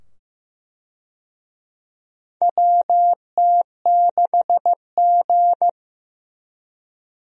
Morse code, heard as a keyed tone, about 15 wpm, 700 Hz. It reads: WT6G
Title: USA Call Signs - 4 Characters - 15wpm